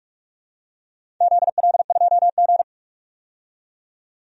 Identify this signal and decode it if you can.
Morse code, heard as a keyed tone, about 46 wpm, 700 Hz. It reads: ZL2G
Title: DXCC-100 Call Signs - 4 Characters - 45wpm